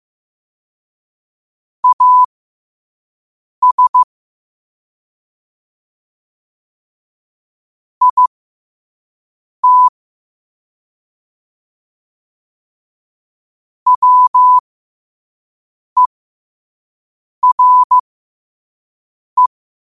Morse code, heard as a keyed tone, about 15 wpm, spaced as for 5 wpm, 1000 Hz. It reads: AS IT WERE
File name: qotdfiles / 5wpm